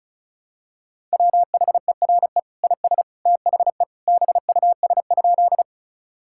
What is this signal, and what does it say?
Morse code, heard as a keyed tone, about 36 wpm, 700 Hz. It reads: WHERE IS THE BUS?
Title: Sentences from Top 1000 Words - Review Part 6 - 35wpm